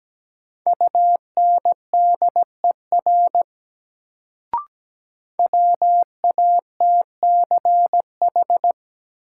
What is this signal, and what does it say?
Morse code, heard as a keyed tone, about 17 wpm, 700 Hz. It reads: UNDER WATCH
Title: Sentences from Top 300 Words - New Word Drill - 17wpm